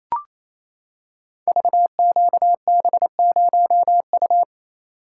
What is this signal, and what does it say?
Morse code, heard as a keyed tone, about 28 wpm, 700 Hz. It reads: VQB0U
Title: Code Groups - 5 Letter:Numbers - 28wpm